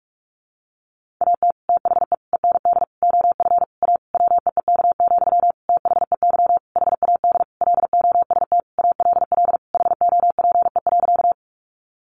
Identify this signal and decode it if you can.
Morse code, heard as a keyed tone, about 44 wpm, 700 Hz. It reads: AT THE END OF A WEEK, THEY HAD LOST ALL HOPE.
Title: Sentences from Top 1400 Words 45wpm